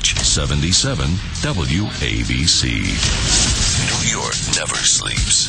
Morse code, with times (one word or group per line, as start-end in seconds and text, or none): none